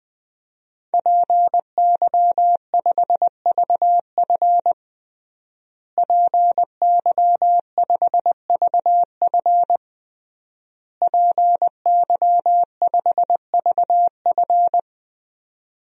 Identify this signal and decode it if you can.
Morse code, heard as a keyed tone, about 20 wpm, 700 Hz. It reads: PY5VF PY5VF PY5VF
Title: DXCC-100 Call Signs - 5 Characters - Repeated 3x - 20wpm